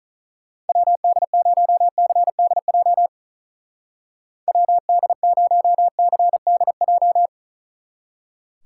Morse code, heard as a keyed tone, about 45 wpm, 700 Hz. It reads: WD0CDJ WD0CDJ